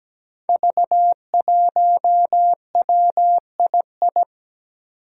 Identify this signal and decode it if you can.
Morse code, heard as a keyed tone, about 17 wpm, 700 Hz. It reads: V1WII
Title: Five Challenging Letter-Number Combinations 17wpm